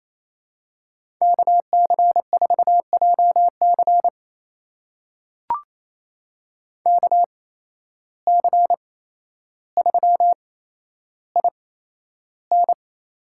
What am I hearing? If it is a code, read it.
Morse code, heard as a keyed tone, about 28 wpm, spaced as for 12 wpm, 700 Hz. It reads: KC4JC KC3IN
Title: USA Call Signs - 8x Character Spacing - 28wpm